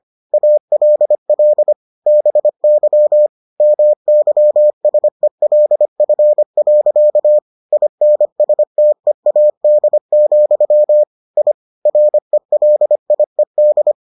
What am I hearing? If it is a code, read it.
Morse code, heard as a keyed tone, about 25 wpm, 600 Hz. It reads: ALL BY MYSELF. INSTEAD, I RELIED